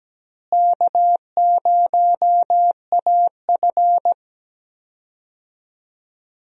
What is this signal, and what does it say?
Morse code, heard as a keyed tone, about 17 wpm, 700 Hz. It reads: K0AF
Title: USA Call Signs - 4 Characters - Repeated 2x - 17wpm